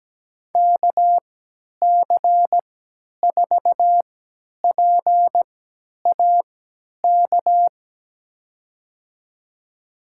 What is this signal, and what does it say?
Morse code, heard as a keyed tone, about 17 wpm, spaced as for 10 wpm, 700 Hz. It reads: KC4PAK
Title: USA Call Signs - 6 Characters - 3x Character Spacing - 17wpm